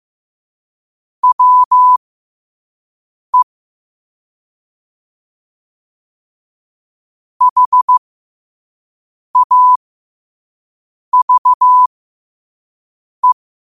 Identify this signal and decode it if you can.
Morse code, heard as a keyed tone, about 15 wpm, spaced as for 5 wpm, 1000 Hz. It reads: WE HAVE
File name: qotdfiles / 5wpm